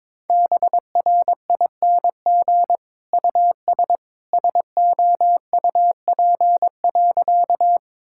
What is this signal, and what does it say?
Morse code, heard as a keyed tone, about 22 wpm, 700 Hz. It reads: BRING US SOUP.
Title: Sentences from Top 1000 Words - Review Part 6 - 22wpm